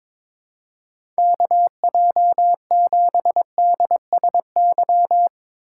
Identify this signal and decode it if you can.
Morse code, heard as a keyed tone, about 23 wpm, 700 Hz. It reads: KJ7DSY
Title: USA Call Signs - 6 Characters - 4x Character Spacing - 22wpm